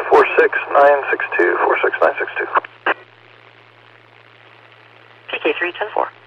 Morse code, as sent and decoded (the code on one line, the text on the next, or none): none